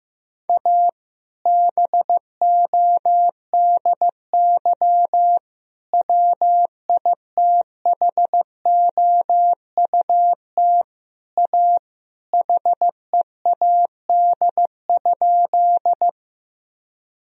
Morse code, heard as a keyed tone, about 15 wpm, 700 Hz. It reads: A BODY WITHOUT A HEAD?
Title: Sentences from Top 1000 Words - Review Part 3 - 15wpm